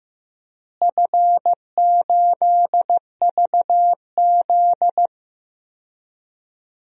Morse code, heard as a keyed tone, about 15 wpm, 700 Hz. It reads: F8VZ
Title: DXCC-100 Call Signs - 4 Characters - Repeated 3x - 15wpm